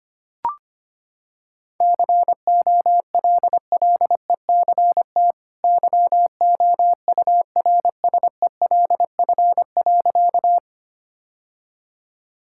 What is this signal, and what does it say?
Morse code, heard as a keyed tone, about 25 wpm, 700 Hz. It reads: COLLECT YOURSELF.
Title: Sentences from Top 1600 Words 25wpm